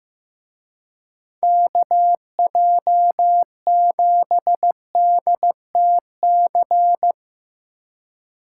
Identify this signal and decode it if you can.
Morse code, heard as a keyed tone, about 16 wpm, 700 Hz. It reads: KJ7DTC
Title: Speed Racing - USA 2x3 Call Signs 15wpm